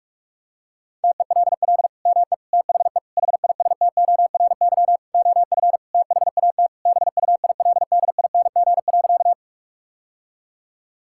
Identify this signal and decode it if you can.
Morse code, heard as a keyed tone, about 42 wpm, 700 Hz. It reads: TELL ME THE HISTORY OF THAT BUILDING.
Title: Sentences from Top 1000 Words - Review Part 5 - 45wpm